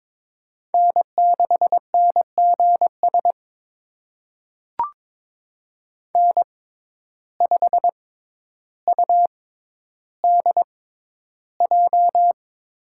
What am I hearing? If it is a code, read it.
Morse code, heard as a keyed tone, about 22 wpm, spaced as for 11 wpm, 700 Hz. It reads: N6NGS N5UDJ
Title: USA Call Signs - 6x Character Spacing - 22wpm